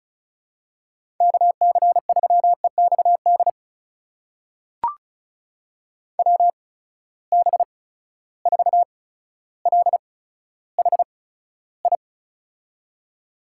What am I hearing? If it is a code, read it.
Morse code, heard as a keyed tone, about 36 wpm, spaced as for 15 wpm, 700 Hz. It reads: KC3EXD WB4LHI
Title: USA Call Signs - 6 Characters - 8x Character Spacing - 35wpm